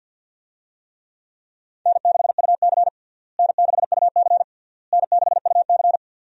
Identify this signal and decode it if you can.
Morse code, heard as a keyed tone, about 53 wpm, 700 Hz. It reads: N6UC N6UC N6UC
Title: USA Call Signs - 4 Characters - Repeated 3x - 50wpm